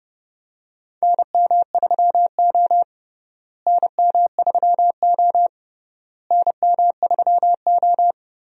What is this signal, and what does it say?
Morse code, heard as a keyed tone, about 30 wpm, 700 Hz. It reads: NM3O NM3O NM3O
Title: USA Call Signs - 4 Characters - Repeated 3x - 30wpm